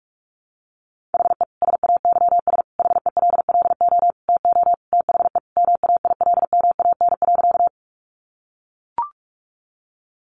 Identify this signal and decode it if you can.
Morse code, heard as a keyed tone, about 44 wpm, 700 Hz. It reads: HE SAYS HELLO TO THE MAILMAN.
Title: Additional Easy Phrases 45wpm